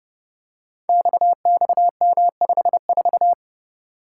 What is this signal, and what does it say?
Morse code, heard as a keyed tone, about 30 wpm, 700 Hz. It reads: XXM54
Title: Code Groups - 5 Letter:Numbers - 30wpm